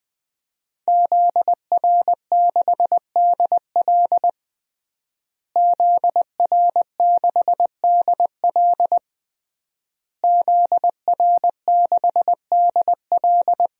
Morse code, heard as a keyed tone, about 20 wpm, 700 Hz. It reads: ZR6DL ZR6DL ZR6DL
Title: DXCC-100 Call Signs - 5 Characters - Repeated 3x - 20wpm